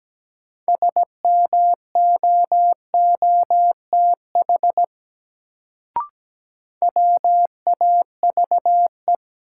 Morse code, heard as a keyed tone, about 17 wpm, 700 Hz. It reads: SMOOTH WAVE
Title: Sentences from Top 1400 Words - New Word Drill - 17wpm